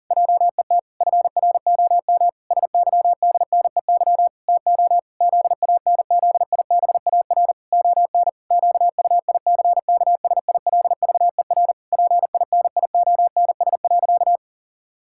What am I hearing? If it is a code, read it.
Morse code, heard as a keyed tone, about 40 wpm, 700 Hz. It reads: JET FROM SYDNEY TO ZANZIBAR ON QUICKSILVER PINIONS.
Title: Pangram Challenge v2 - 40wpm